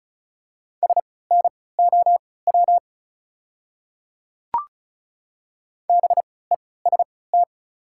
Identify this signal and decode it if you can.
Morse code, heard as a keyed tone, about 36 wpm, spaced as for 20 wpm, 690 Hz. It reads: SNOW BEST
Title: 3x Character Spacing - Top 500 Words - 35wpm